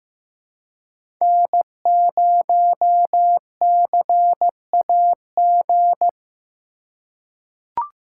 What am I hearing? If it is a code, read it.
Morse code, heard as a keyed tone, about 15 wpm, 700 Hz. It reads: N0CAG
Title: USA Call Signs - 5 Characters - 3x Character Spacing - 15wpm